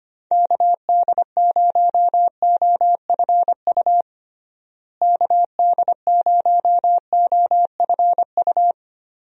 Morse code, heard as a keyed tone, about 25 wpm, 700 Hz. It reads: KD0OFU KD0OFU
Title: USA Call Signs - 6 Characters - Repeated 2x - 25wpm